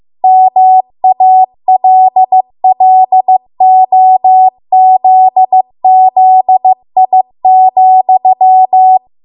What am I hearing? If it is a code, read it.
Morse code, heard as a keyed tone, about 15 wpm, 750 Hz. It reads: MALLOZZI,